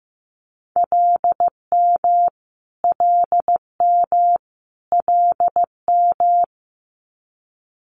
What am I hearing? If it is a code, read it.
Morse code, heard as a keyed tone, about 15 wpm, 700 Hz. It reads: LM LM LM
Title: Multi-letter Phonemes Repeated 3x 15wpm